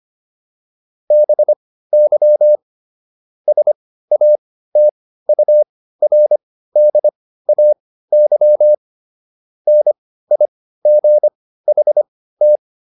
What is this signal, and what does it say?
Morse code, heard as a keyed tone, about 25 wpm, spaced as for 15 wpm, 600 Hz. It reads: BY SATURDAY NIGHT